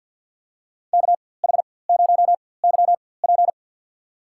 Morse code, heard as a keyed tone, about 49 wpm, spaced as for 23 wpm, 700 Hz. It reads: KH0YP
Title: USA Call Signs - 4x Character Spacing - 50wpm